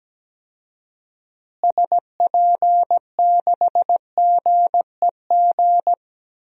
Morse code, heard as a keyed tone, about 17 wpm, 700 Hz. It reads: SP6GEG
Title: DXCC-100 Call Signs - 6 Characters - Repeated 3x - 17wpm